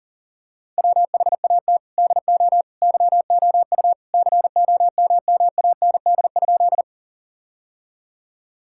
Morse code, heard as a keyed tone, about 40 wpm, 700 Hz. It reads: WHAT DO YOU COMMAND?
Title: Sentences from Top 1600 Words 40wpm